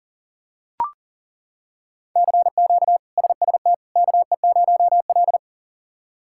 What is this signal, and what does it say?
Morse code, heard as a keyed tone, about 40 wpm, 700 Hz. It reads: CQ SST KE0L
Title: K1USN SST QSO Practice - 40wpm